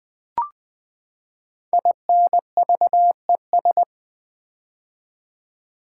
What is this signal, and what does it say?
Morse code, heard as a keyed tone, about 20 wpm, 700 Hz. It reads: INVES
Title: Build A Word - 5 Letters At A Time - 10-Letter Words - 20wpm